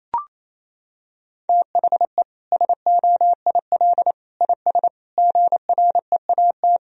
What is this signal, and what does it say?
Morse code, heard as a keyed tone, about 28 wpm, 700 Hz. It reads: THE SOIL IS GREAT
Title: Sentences from Top 1000 Words - Review Part 4 - 28wpm